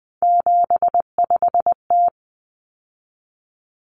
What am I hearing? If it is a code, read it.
Morse code, heard as a keyed tone, about 20 wpm, 700 Hz. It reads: 75T
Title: Common Cut Numbers - 3-Digit 20wpm